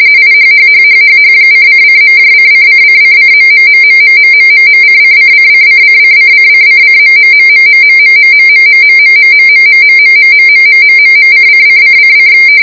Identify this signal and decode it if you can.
RTTY, about 45.45 baud. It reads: IHHHHHH:::.::::::.........:::::::::::::..........................::::::